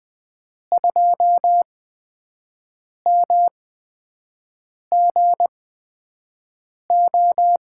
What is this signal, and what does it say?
Morse code, heard as a keyed tone, about 20 wpm, spaced as for 5 wpm, 700 Hz. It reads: 2MGO
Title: USA Call Signs - 8x Character Spacing - 20wpm